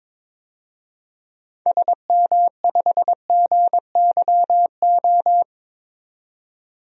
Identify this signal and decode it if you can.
Morse code, heard as a keyed tone, about 21 wpm, 700 Hz. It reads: SM5GYO
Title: DXCC-100 Call Signs - 6 Characters - 22wpm